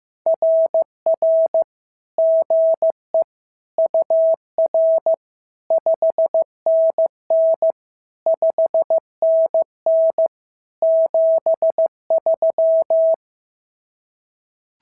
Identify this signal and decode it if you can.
Morse code, heard as a keyed tone, about 15 wpm, 690 Hz. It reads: RR GE UR 5NN 5NN 73
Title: SOTA QSO Practice 15wpm